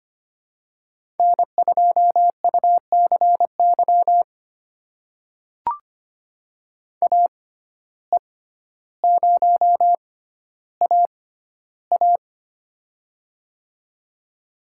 Morse code, heard as a keyed tone, about 25 wpm, spaced as for 13 wpm, 700 Hz. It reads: N2UCY AE0AA
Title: USA Call Signs - 5 Characters - 6x Character Spacing - 25wpm